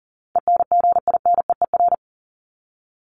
Morse code, heard as a keyed tone, about 40 wpm, 700 Hz. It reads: ENGINEER
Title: ING Suffix - Top 100 Words with Base Words 40wpm